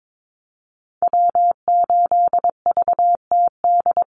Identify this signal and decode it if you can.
Morse code, heard as a keyed tone, about 22 wpm, 700 Hz. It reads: W8VTD
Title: USA Call Signs Repeated 3x 22wpm